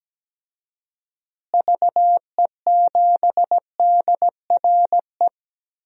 Morse code, heard as a keyed tone, about 17 wpm, 700 Hz. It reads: VE7DRE
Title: DXCC-100 Call Signs - 6 Characters - Repeated 2x - 17wpm